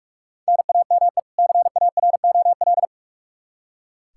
Morse code, heard as a keyed tone, about 46 wpm, 700 Hz. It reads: NAME CAROL
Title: Practice QSO - Exchange 1 - Beginning - 45wpm